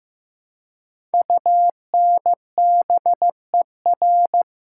Morse code, heard as a keyed tone, about 15 wpm, 700 Hz. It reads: UNBER
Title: Code Groups - 5 Letters - 15wpm